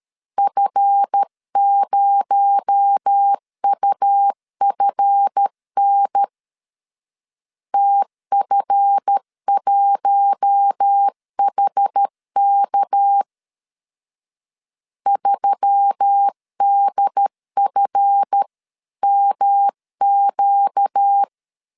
Morse code, heard as a keyed tone, about 13 wpm, 800 Hz. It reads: F0UFN TF1HK 3DFMQ
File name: LEZIONI / 7 GF9Z / 7 GF 12wpm 2122